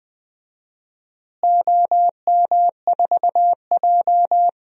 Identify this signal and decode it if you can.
Morse code, heard as a keyed tone, about 20 wpm, 700 Hz. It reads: OM4J
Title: DXCC-100 Call Signs - 4 Characters - Repeated 3x - 20wpm